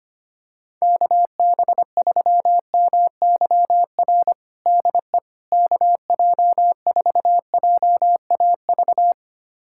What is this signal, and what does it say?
Morse code, heard as a keyed tone, about 25 wpm, 700 Hz. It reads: KB3MYR DE KJ4JAV
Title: USA Callsign DE Callsign - 6 Characters - 25wpm